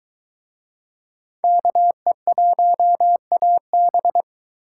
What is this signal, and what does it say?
Morse code, heard as a keyed tone, about 22 wpm, 700 Hz. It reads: KE1AB